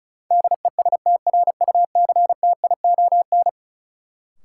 Morse code, heard as a keyed tone, about 34 wpm, 700 Hz. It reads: DESTRUCTION
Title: Sets of 5 Words Encore - New Word Drill - 35wpm